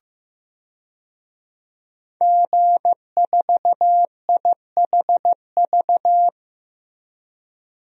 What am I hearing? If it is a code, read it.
Morse code, heard as a keyed tone, about 15 wpm, 700 Hz. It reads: G4IHV
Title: DXCC-100 Call Signs - 5 Characters - Repeated 2x - 15wpm